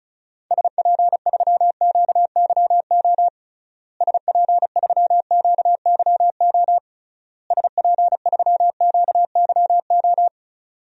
Morse code, heard as a keyed tone, about 35 wpm, 700 Hz. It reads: SP3QYO SP3QYO SP3QYO
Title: DXCC-100 Call Signs - 6 Characters - Repeated 3x - 35wpm